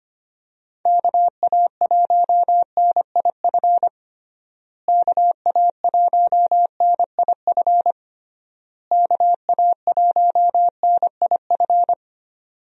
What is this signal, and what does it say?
Morse code, heard as a keyed tone, about 25 wpm, 700 Hz. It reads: KA1NIF KA1NIF KA1NIF
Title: USA Call Signs - 6 Characters - Repeated 3x - 25wpm